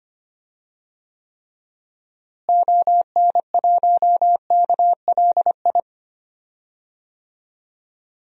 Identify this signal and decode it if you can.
Morse code, heard as a keyed tone, about 25 wpm, 700 Hz. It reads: ON1KLI
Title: DXCC-100 Call Signs - 6 Characters - Repeated 3x - 25wpm